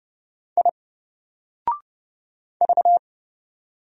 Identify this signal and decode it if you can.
Morse code, heard as a keyed tone, about 29 wpm, 700 Hz. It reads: I V
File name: Single-Letter-Number-30wpm